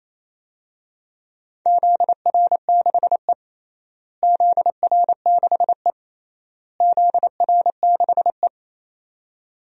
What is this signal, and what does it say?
Morse code, heard as a keyed tone, about 28 wpm, 700 Hz. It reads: ZR6E ZR6E ZR6E
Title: DXCC-100 Call Signs - 4 Characters - Repeated 3x - 28wpm